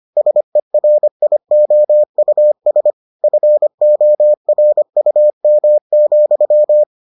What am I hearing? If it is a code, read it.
Morse code, heard as a keyed tone, about 25 wpm, 600 Hz. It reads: SERIOUS FORUM,